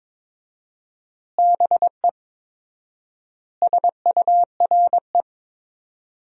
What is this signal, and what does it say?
Morse code, heard as a keyed tone, about 21 wpm, 700 Hz. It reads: BE SURE
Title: 4x Word-Spacing - Sentences from Top 300 Words - 22wpm